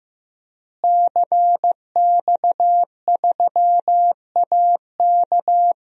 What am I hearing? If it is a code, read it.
Morse code, heard as a keyed tone, about 15 wpm, 700 Hz. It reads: CX3AK
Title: DXCC-100 Call Signs - 5 Characters - Repeated 2x - 15wpm